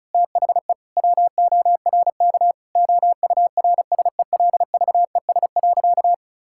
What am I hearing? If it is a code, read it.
Morse code, heard as a keyed tone, about 34 wpm, 700 Hz. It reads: THE WORK OURSELVES.
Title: Sentences from Top 1600 Words 35wpm